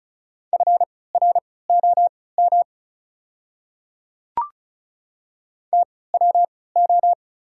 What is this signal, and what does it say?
Morse code, heard as a keyed tone, about 36 wpm, spaced as for 20 wpm, 690 Hz. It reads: FROM TWO
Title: 3x Character Spacing - Top 100 Words - 35wpm